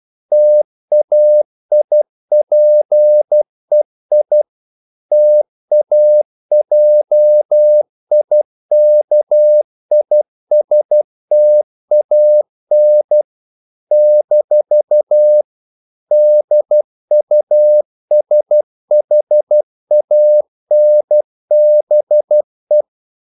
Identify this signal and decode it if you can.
Morse code, heard as a keyed tone, about 12 wpm, 600 Hz. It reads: TAIPEI TAJIKISTAN - DUSHANBE
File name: CW / Practice / NationalCapitals12